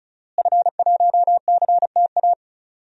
Morse code, heard as a keyed tone, about 36 wpm, 700 Hz. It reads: F1CTA
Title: DXCC-100 Call Signs - 5 Characters - Repeated 2x - 35wpm